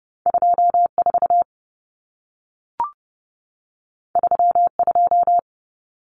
Morse code, heard as a keyed tone, about 30 wpm, 700 Hz. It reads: 24 32
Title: Numbers - 2-Digit 30wpm